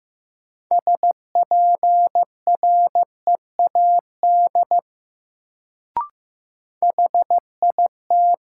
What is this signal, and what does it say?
Morse code, heard as a keyed tone, about 15 wpm, 700 Hz. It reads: SPREAD HIT
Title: Sentences from Top 1200 Words - New Word Drill - 15wpm